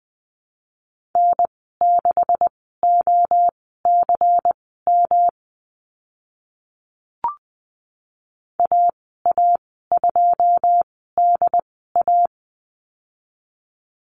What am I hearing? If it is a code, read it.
Morse code, heard as a keyed tone, about 20 wpm, spaced as for 14 wpm, 700 Hz. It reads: N6OCM AA2DA
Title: 154 Lesson 25g - Letter p - Call Signs 20wpm